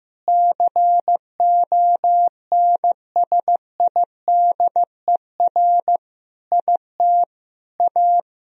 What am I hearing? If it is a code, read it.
Morse code, heard as a keyed tone, about 16 wpm, 700 Hz. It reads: CONSIDER IT A
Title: Sentences from Top 1000 Words - Review Part 4 - 15wpm